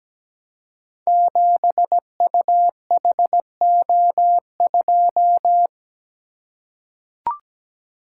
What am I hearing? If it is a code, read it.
Morse code, heard as a keyed tone, about 17 wpm, 700 Hz. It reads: 7UHO2
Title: Code Groups - 5 Letter:Numbers - 17wpm